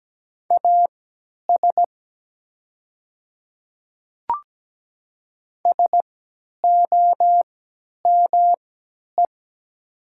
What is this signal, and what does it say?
Morse code, heard as a keyed tone, about 17 wpm, spaced as for 10 wpm, 700 Hz. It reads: AS SOME
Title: 3x Character Spacing - Top 100 Words - 17wpm